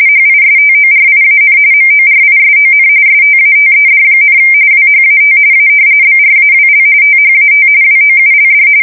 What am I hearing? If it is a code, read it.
RTTY, about 45.45 baud. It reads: HISTORIAN MORISON STATED, WHAT A BRAWL THAT WOULD
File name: fsk170 / battleshipnj